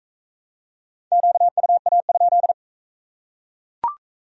Morse code, heard as a keyed tone, about 40 wpm, 700 Hz. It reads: QUA?